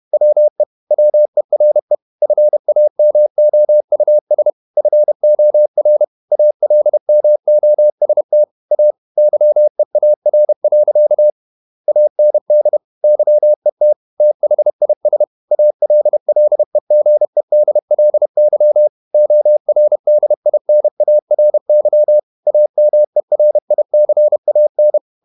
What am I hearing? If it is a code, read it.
Morse code, heard as a keyed tone, about 31 wpm, 600 Hz. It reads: WE WERE FAMOUS FOR ALMOST A YEAR. AND YET THIS ALLEGEDLY ORDINARY AMERICAN
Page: Never 31wpm | CW med Gnister